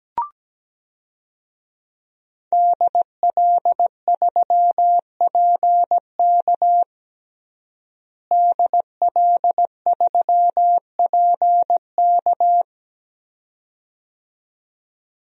Morse code, heard as a keyed tone, about 17 wpm, 700 Hz. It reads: DL3PK DL3PK
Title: DXCC-100 Call Signs - 5 Characters - Repeated 2x - 17wpm